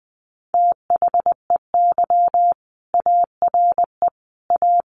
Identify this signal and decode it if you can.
Morse code, heard as a keyed tone, about 20 wpm, 700 Hz. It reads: THEY ARE A
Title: Sentences from Top 1200 Words 20wpm